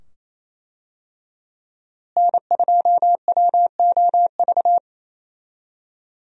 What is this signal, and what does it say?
Morse code, heard as a keyed tone, about 28 wpm, 700 Hz. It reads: N2WOV
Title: USA Call Signs - 5 Characters - Repeated 2x - 28wpm